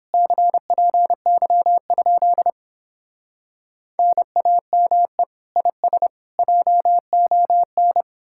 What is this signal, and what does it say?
Morse code, heard as a keyed tone, about 31 wpm, 700 Hz. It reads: CPY? NAME IS JON